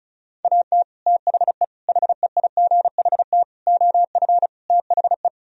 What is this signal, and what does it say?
Morse code, heard as a keyed tone, about 43 wpm, 700 Hz. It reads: AT THE HEIGHT OF THE
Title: Speed-Racing - Sets of 5 Words 35wpm